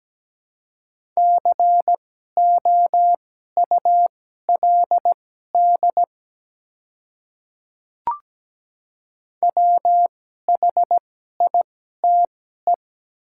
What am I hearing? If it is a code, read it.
Morse code, heard as a keyed tone, about 17 wpm, spaced as for 12 wpm, 700 Hz. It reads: COULD WHITE
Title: 2x Character Spacing - Top 300 Words - 17wpm